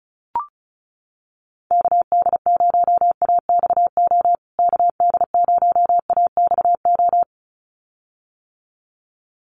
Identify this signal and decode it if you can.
Morse code, heard as a keyed tone, about 34 wpm, 700 Hz. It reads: KD0AXO KD0AXO
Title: USA Call Signs Repeated 2x 35wpm